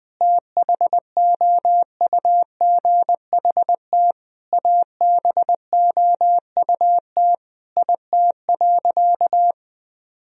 Morse code, heard as a keyed tone, about 20 wpm, 700 Hz. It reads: THOUGHT ABOUT IT.
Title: Sentences from Top 1000 Words - Review Part 1 - 20wpm